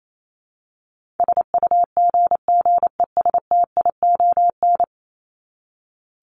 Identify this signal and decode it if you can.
Morse code, heard as a keyed tone, about 28 wpm, 700 Hz. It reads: SUGGESTION
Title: Top 1501-1750 Words 28wpm